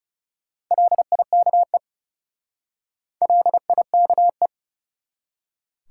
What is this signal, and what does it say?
Morse code, heard as a keyed tone, about 39 wpm, 700 Hz. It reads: LIKE LIKE